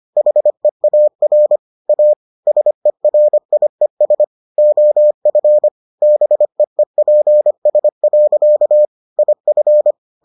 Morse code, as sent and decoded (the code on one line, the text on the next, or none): .... .- .-. / .- / ... .-. .. ... / --- ..-. / -... . .--. ... .-.-.- / .. ..-.
HEAR A SERIES OF BEEPS. IF